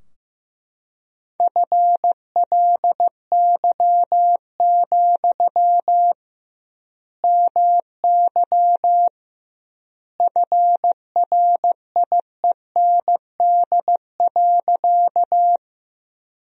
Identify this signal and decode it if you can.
Morse code, heard as a keyed tone, about 15 wpm, 700 Hz. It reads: FLY, MY FRIEND.
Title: 2x Word-Spacing - Sentences from Top 500 Words - 15wpm